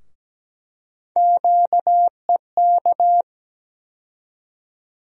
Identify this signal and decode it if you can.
Morse code, heard as a keyed tone, about 17 wpm, 700 Hz. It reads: QEK
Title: USA Call Sign Postfixes - 3 Letters 17wpm